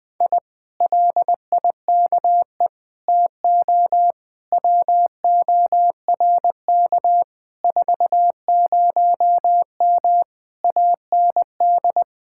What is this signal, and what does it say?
Morse code, heard as a keyed tone, about 20 wpm, 700 Hz. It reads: I LIKE TO WORK 40M AND
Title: Speed-Racing Common QSO Phrases - 17wpm